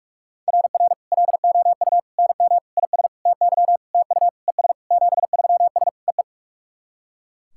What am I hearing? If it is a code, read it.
Morse code, heard as a keyed tone, about 48 wpm, 700 Hz. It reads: RR LOU NM IS TY TU ES 73S EE
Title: SOTA QSO Practice 45wpm